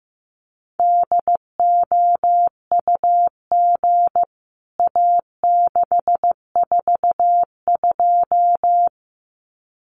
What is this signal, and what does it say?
Morse code, heard as a keyed tone, about 15 wpm, 700 Hz. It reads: DOUG A642
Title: CWT Exchange 15wpm